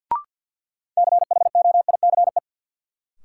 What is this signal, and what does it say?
Morse code, heard as a keyed tone, about 53 wpm, 700 Hz. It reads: CHOICE
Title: Sets of 5 Words - New Word Drill - 50wpm